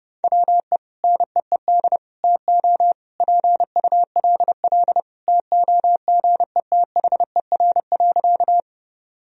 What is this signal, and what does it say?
Morse code, heard as a keyed tone, about 32 wpm, 700 Hz. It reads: WE NEED TO PULL TOGETHER.
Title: Sentences from Top 1000 Words - Review Part 2 - 30wpm